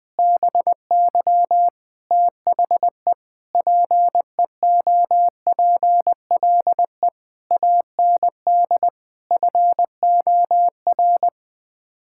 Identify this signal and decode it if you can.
Morse code, heard as a keyed tone, about 20 wpm, 700 Hz. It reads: BY THE PEOPLE AND FOR
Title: Sets of 8 Words - 20wpm